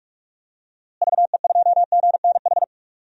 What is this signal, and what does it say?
Morse code, heard as a keyed tone, about 46 wpm, 700 Hz. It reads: VE2GNH
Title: DXCC-100 Call Signs - 6 Characters - Repeated 3x - 45wpm